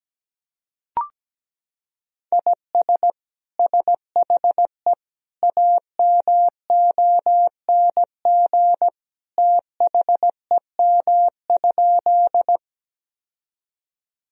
Morse code, heard as a keyed tone, about 17 wpm, 700 Hz. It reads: IS SHE AMONG THEM?
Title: Sentences from Top 1000 Words - Review Part 3 - 17wpm